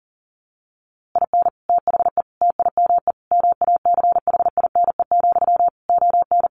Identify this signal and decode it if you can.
Morse code, heard as a keyed tone, about 40 wpm, 700 Hz. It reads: IN THE TIME MACHINE, ON
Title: A Sound Of Thunder 40wpm